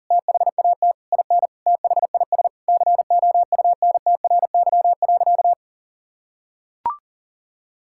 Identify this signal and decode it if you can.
Morse code, heard as a keyed tone, about 40 wpm, 700 Hz. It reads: THAT IN THIS COUNTRY.
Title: Sentences from Top 1000 Words - Review Part 5 - 40wpm